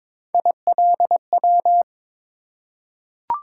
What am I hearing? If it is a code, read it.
Morse code, heard as a keyed tone, about 22 wpm, 700 Hz. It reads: ILW
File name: practice / 22wpm / Three Letter-Number Combinations - 22wpm